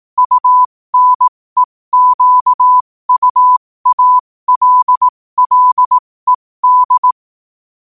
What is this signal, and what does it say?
Morse code, heard as a keyed tone, about 18 wpm, spaced as for 15 wpm, 1000 Hz. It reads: UNEQUALLED
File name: qotdfiles / 15wpm